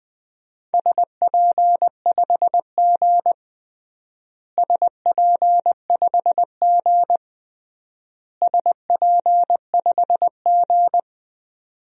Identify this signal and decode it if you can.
Morse code, heard as a keyed tone, about 20 wpm, 700 Hz. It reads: SP5G SP5G SP5G
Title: DXCC-100 Call Signs - 4 Characters - Repeated 3x - 20wpm